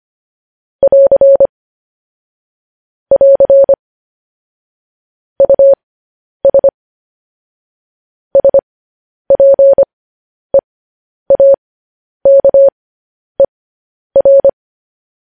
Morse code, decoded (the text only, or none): <AR> <AR> US SPEAKER